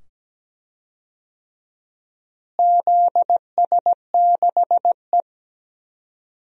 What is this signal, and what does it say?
Morse code, heard as a keyed tone, about 17 wpm, 700 Hz. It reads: ZS6E
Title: DXCC-100 Call Signs - 4 Characters - Repeated 2x - 17wpm